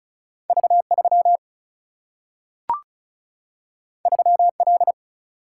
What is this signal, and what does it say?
Morse code, heard as a keyed tone, about 36 wpm, 700 Hz. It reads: V3 3L
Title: Single-Letter Single-Number 35wpm